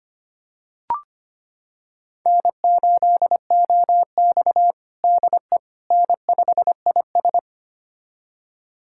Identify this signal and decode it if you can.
Morse code, heard as a keyed tone, about 25 wpm, 700 Hz. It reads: N8OX DE N5IS
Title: USA Callsign DE Callsign - 4 Characters - 25wpm